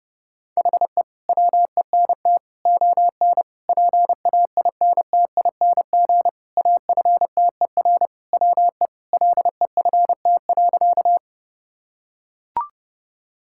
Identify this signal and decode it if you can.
Morse code, heard as a keyed tone, about 32 wpm, 700 Hz. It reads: HE WENT ON PAINTING AFTER WE LEFT.
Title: Sentences from Top 1000 Words - Review Part 6 - 30wpm